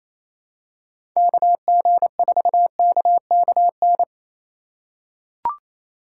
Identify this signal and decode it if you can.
Morse code, heard as a keyed tone, about 28 wpm, 700 Hz. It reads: KG4KKN
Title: USA Call Signs - 6 Characters - 3x Character Spacing - 28wpm